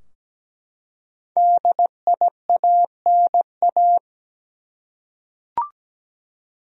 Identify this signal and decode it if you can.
Morse code, heard as a keyed tone, about 17 wpm, 700 Hz. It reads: DIANA
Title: USA Common Names - 17wpm